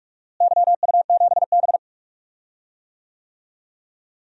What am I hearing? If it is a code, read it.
Morse code, heard as a keyed tone, about 46 wpm, 700 Hz. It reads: YU7B
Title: DXCC-100 Call Signs - 4 Characters - Repeated 3x - 45wpm